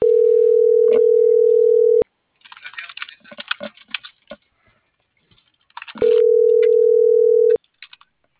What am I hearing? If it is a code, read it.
Ringback tone (North America), 440 + 480 Hz.